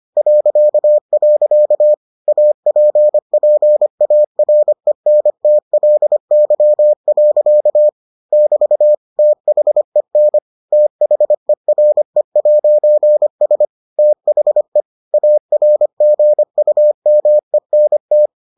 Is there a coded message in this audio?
Yes: Morse code from 0.2 to 18.3 s, heard as a keyed tone, about 25 wpm, 600 Hz. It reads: .. APPARENTLY. <BT> THEN THERE'S THE ARGUMENT